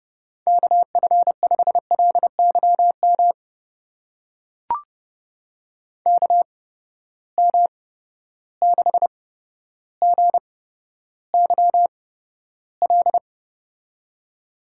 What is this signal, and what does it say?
Morse code, heard as a keyed tone, about 31 wpm, spaced as for 13 wpm, 700 Hz. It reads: KF5LYM KM6GYL
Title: USA Call Signs - 6 Characters - 8x Character Spacing - 30wpm